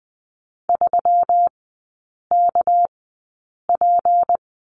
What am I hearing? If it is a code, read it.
Morse code, heard as a keyed tone, about 20 wpm, 700 Hz. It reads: 3 K P
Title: 198 Lesson 32e - Letter k - Three Characters 20wpm